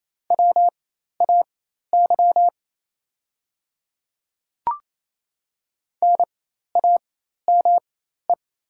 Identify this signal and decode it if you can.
Morse code, heard as a keyed tone, about 28 wpm, spaced as for 13 wpm, 690 Hz. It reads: WAY NAME
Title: 4x Character Spacing - Top 200 Words - 28wpm